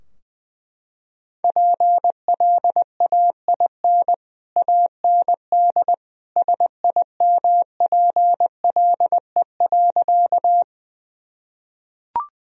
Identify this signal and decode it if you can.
Morse code, heard as a keyed tone, about 20 wpm, 700 Hz. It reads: PLAIN AND SIMPLE.
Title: Sentences from Top 1000 Words - Review Part 2 - 20wpm